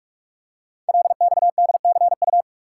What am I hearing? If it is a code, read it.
Morse code, heard as a keyed tone, about 44 wpm, 700 Hz. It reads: LXDCU
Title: Code Groups - 5 Letters - 45wpm